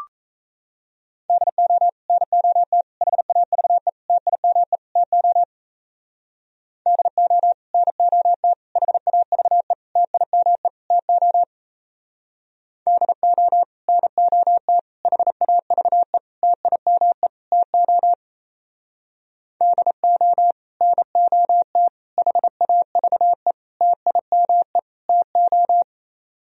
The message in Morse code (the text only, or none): DO NOT HAVE TIME TO DO NOT HAVE TIME TO DO NOT HAVE TIME TO DO NOT HAVE TIME TO